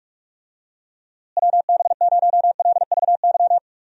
Morse code, heard as a keyed tone, about 42 wpm, 700 Hz. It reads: WB0LVY
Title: USA Call Signs - 6 Characters - Repeated 2x - 45wpm